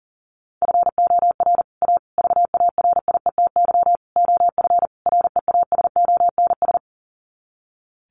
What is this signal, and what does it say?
Morse code, heard as a keyed tone, about 40 wpm, 700 Hz. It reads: FOR A VARIETY OF REASONS
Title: Sets of 5 Words Encore 40wpm